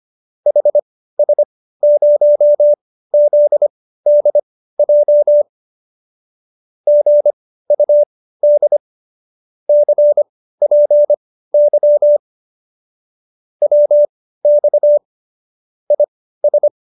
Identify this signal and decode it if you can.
Morse code, heard as a keyed tone, about 25 wpm, spaced as for 15 wpm, 590 Hz. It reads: HS0ZDJ GUD CPY WX IS